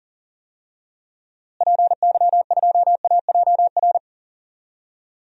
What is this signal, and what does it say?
Morse code, heard as a keyed tone, about 40 wpm, 700 Hz. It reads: PY2AJR
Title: DXCC-100 Call Signs - 6 Characters - Repeated 3x - 40wpm